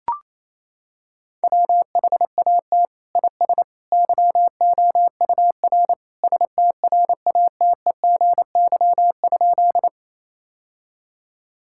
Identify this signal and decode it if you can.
Morse code, heard as a keyed tone, about 28 wpm, 700 Hz. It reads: WHAT IS YOUR STRATEGY?
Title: Sentences from Top 1000 Words - Review Part 5 - 28wpm